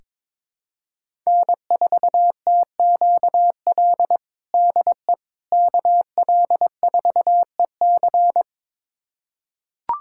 Morse code, heard as a keyed tone, about 22 wpm, 700 Hz. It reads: N4TQL DE KL4EC
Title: USA Callsign DE Callsign - 5 Characters - 22wpm